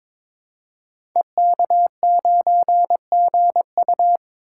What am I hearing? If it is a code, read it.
Morse code, heard as a keyed tone, about 22 wpm, 700 Hz. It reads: EK9GU
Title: Code Groups - 5 Letter:Numbers - 22wpm